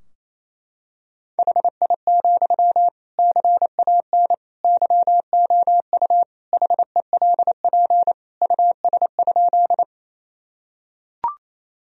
Morse code, heard as a keyed tone, about 28 wpm, 700 Hz. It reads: HI, CAN YOU HELP US?
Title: Sentences from Top 1000 Words - Review Part 3 - 28wpm